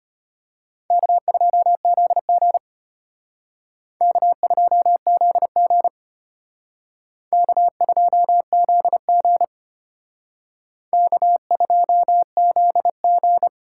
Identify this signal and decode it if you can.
Morse code, heard as a keyed tone, about 32 wpm, 700 Hz. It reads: K2ZG K2ZG K2ZG K2ZG